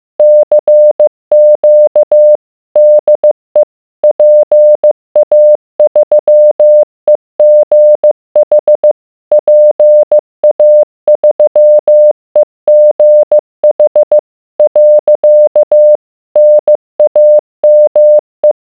Morse code, heard as a keyed tone, about 15 wpm, 600 Hz. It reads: CQ DE PA3EGH PA3EGH . NAME